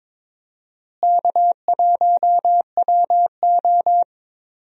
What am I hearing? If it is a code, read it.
Morse code, heard as a keyed tone, about 22 wpm, 700 Hz. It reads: K1WO
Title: USA Call Signs - 4 Characters - Repeated 2x - 22wpm